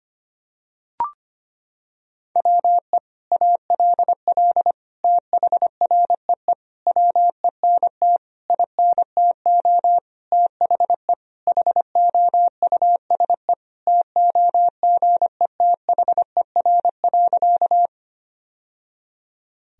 Morse code, heard as a keyed tone, about 25 wpm, 700 Hz. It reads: WE ALL THREE WENT INTO THE HOUSE TOGETHER.
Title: Sentences from Top 1000 Words - Review Part 1 - 25wpm